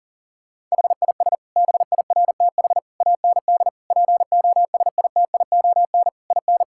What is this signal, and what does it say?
Morse code, heard as a keyed tone, about 40 wpm, 700 Hz. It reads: HIS BIRTH AND POSITION IN
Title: Sentences from Top 1000 Words - Review Part 5 - 40wpm